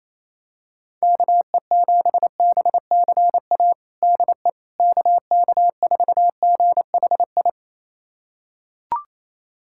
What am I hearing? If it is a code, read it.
Morse code, heard as a keyed tone, about 28 wpm, 700 Hz. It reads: KE7BCA DE KK4GHI
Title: USA Callsign DE Callsign 28wpm